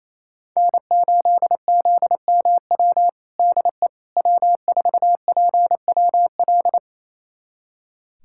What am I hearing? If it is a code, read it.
Morse code, heard as a keyed tone, about 28 wpm, 700 Hz. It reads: N8ZMW DE W4PWL
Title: USA Callsign DE Callsign - 5 Characters - 28wpm